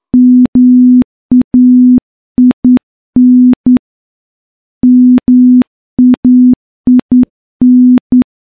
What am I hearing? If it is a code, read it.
Morse code, heard as a keyed tone, about 10 wpm, 250 Hz. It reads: MAIN MAIN
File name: morse / morse-training-10WPM